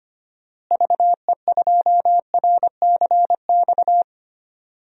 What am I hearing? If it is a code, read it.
Morse code, heard as a keyed tone, about 25 wpm, 700 Hz. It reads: VE2RCX
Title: DXCC-100 Call Signs - 6 Characters - Repeated 3x - 25wpm